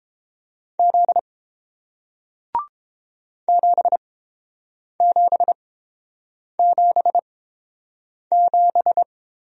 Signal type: Morse code

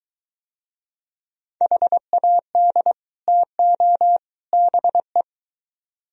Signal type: Morse code

